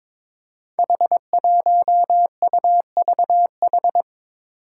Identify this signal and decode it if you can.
Morse code, heard as a keyed tone, about 22 wpm, 700 Hz. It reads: H1UVH
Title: Five Challenging Letter-Number Combinations 22wpm